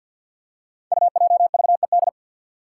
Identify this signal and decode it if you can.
Morse code, heard as a keyed tone, about 53 wpm, 700 Hz. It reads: UJVED
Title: Five Challenging Letter-Number Combinations 50wpm